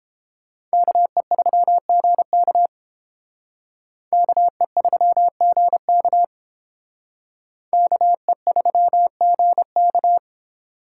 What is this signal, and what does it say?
Morse code, heard as a keyed tone, about 29 wpm, 700 Hz. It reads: KE3GK KE3GK KE3GK